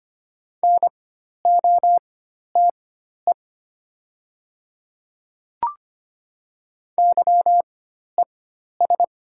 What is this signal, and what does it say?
Morse code, heard as a keyed tone, about 25 wpm, spaced as for 12 wpm, 700 Hz. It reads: NOTE YES
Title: 4x Character Spacing - Top 500 Words - 25wpm